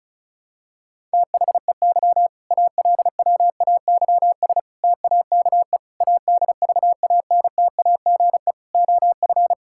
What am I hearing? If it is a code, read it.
Morse code, heard as a keyed tone, about 34 wpm, 700 Hz. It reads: THEY ALWAYS TAKE ADVANTAGE OF
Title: Sentences from Top 1000 Words - Review Part 6 - 35wpm